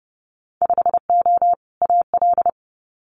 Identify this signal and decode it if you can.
Morse code, heard as a keyed tone, about 30 wpm, 700 Hz. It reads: 5O AL
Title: Winter Field Day Exchanges 30wpm